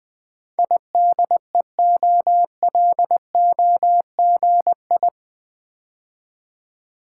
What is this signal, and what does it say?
Morse code, heard as a keyed tone, about 20 wpm, 700 Hz. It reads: IDEOLOGI
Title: Build A Word - 4 Letters At A Time - 11-Letter Words - 20wpm